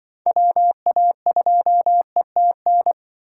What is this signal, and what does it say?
Morse code, heard as a keyed tone, about 24 wpm, 700 Hz. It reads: WA2ETN
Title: Speed Racing - USA 2x3 Call Signs 20wpm